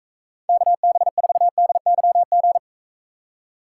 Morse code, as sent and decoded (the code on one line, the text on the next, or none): -.- -... ....- -.. -.-- --.
KB4DYG